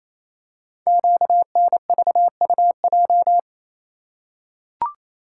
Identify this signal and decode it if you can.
Morse code, heard as a keyed tone, about 29 wpm, 700 Hz. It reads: QNVUJ
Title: Code Groups - 5 Letters - 28wpm